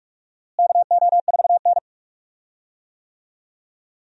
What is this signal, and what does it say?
Morse code, heard as a keyed tone, about 46 wpm, 700 Hz. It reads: KO4N
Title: USA Call Signs - 4 Characters - Repeated 3x - 45wpm